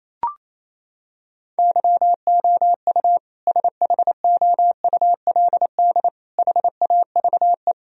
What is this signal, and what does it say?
Morse code, heard as a keyed tone, about 28 wpm, 700 Hz. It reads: YOU SHOULD HAVE
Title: Sentences from Top 1000 Words - Review Part 4 - 28wpm